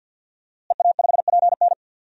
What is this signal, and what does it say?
Morse code, heard as a keyed tone, about 47 wpm, 700 Hz. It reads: EA5PN
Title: DXCC-100 Call Signs - 5 Characters - Repeated 2x - 50wpm